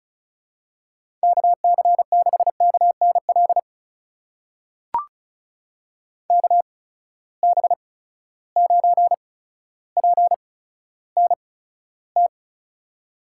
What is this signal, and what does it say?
Morse code, heard as a keyed tone, about 36 wpm, spaced as for 15 wpm, 700 Hz. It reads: KC6KNL KB9PNT
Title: USA Call Signs - 8x Character Spacing - 35wpm